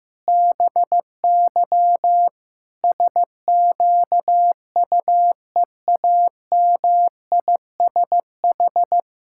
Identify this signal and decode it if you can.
Morse code, heard as a keyed tone, about 15 wpm, 700 Hz. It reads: BY SQUEAMISH
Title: Pangram Challenge v2 - 15wpm